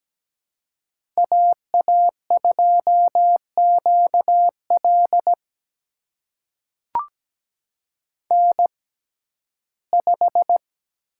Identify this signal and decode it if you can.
Morse code, heard as a keyed tone, about 17 wpm, spaced as for 12 wpm, 700 Hz. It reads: AA2QL N5
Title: USA Call Signs - 5 Characters - 6x Character Spacing - 17wpm